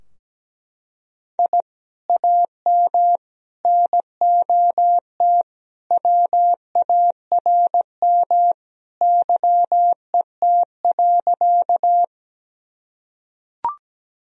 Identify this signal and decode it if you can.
Morse code, heard as a keyed tone, about 17 wpm, 700 Hz. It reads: I AM NOT WARM YET.
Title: Sentences from Top 1000 Words - Review Part 2 - 17wpm